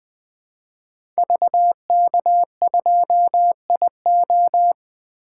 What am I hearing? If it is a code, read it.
Morse code, heard as a keyed tone, about 20 wpm, 700 Hz. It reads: VK2IO
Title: 2023 POTA Activator Call Signs - 20wpm